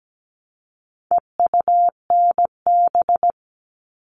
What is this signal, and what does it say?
Morse code, heard as a keyed tone, about 17 wpm, 700 Hz. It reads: EUNB
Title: All Cut Numbers - 4-Digit 17wpm